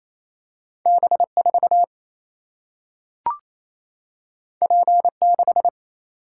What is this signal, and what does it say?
Morse code, heard as a keyed tone, about 28 wpm, 700 Hz. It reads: B4 P6
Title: Single-Letter Single-Number 28wpm